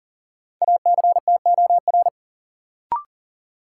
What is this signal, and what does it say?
Morse code, heard as a keyed tone, about 38 wpm, 700 Hz. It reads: ACTOR
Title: Sentences from Top 1500 Words - New Word Drill - 40wpm